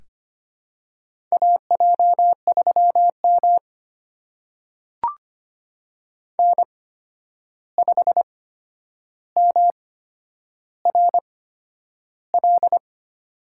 Morse code, heard as a keyed tone, about 25 wpm, spaced as for 10 wpm, 700 Hz. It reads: AJ3M N5MRL
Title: USA Call Signs - 8x Character Spacing - 25wpm